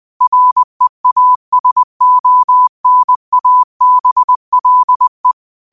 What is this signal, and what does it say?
Morse code, heard as a keyed tone, about 20 wpm, 1000 Hz. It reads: REASONABLE
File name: qotdfiles / 20wpm